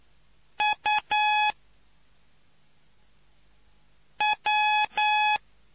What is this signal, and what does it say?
Morse code, heard as a keyed tone, about 9 wpm, 850 Hz. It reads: U W